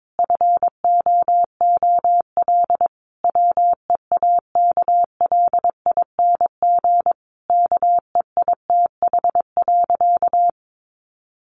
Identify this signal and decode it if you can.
Morse code, heard as a keyed tone, about 22 wpm, 700 Hz. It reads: FOOL WEAKLING KEITH.
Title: A Sound Of Thunder 22wpm